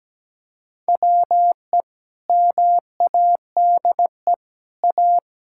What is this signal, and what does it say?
Morse code, heard as a keyed tone, about 17 wpm, 700 Hz. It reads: WE MADE A
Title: Sentences from Top 1600 Words 17wpm